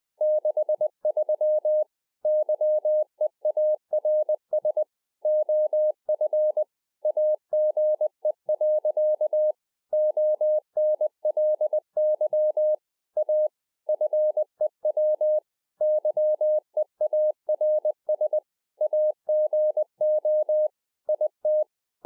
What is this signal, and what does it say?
Morse code, heard as a keyed tone, about 20 wpm, 610 Hz. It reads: 63 YEARS OF AGE. ONLY A FEW YEARS AGO IT
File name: 20WPM